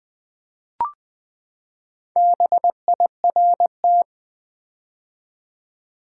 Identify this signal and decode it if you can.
Morse code, heard as a keyed tone, about 20 wpm, 700 Hz. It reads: BIRT
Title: Build A Word - 4 Letters At A Time - 8-Letter Words - 20wpm